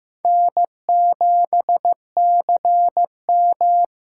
Morse code, heard as a keyed tone, about 15 wpm, 700 Hz. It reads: N7CM
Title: USA Call Signs - 4 Characters - 4x Character Spacing - 15wpm